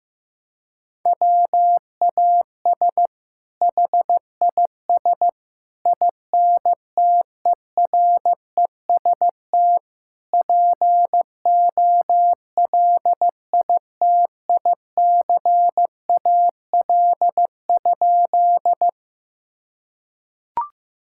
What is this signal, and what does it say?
Morse code, heard as a keyed tone, about 15 wpm, 700 Hz. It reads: WAS HIS INTEREST POLITICAL?
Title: Sentences from Top 1000 Words - Review Part 5 - 15wpm